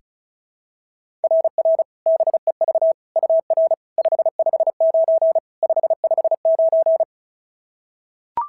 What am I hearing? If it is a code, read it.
Morse code, heard as a keyed tone, about 36 wpm, 690 Hz. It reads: RR BEV UR 559 559
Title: SOTA QSO Practice 35wpm